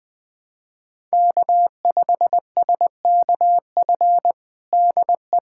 Morse code, heard as a keyed tone, about 21 wpm, 700 Hz. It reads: K5SKF DE
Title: USA Callsign DE Callsign - 5 Characters - 20wpm